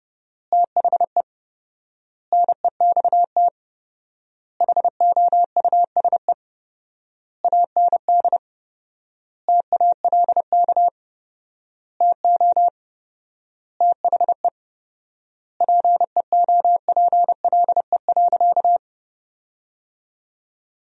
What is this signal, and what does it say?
Morse code, heard as a keyed tone, about 30 wpm, 700 Hz. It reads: THE NEXT HOUSE AND TALK TO THE PEOPLE.
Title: 4x Word-Spacing - Sentences from Top 400 Words - 30wpm